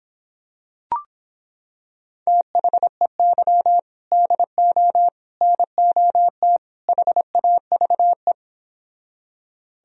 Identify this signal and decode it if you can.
Morse code, heard as a keyed tone, about 26 wpm, 700 Hz. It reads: THEY DO NOT HAVE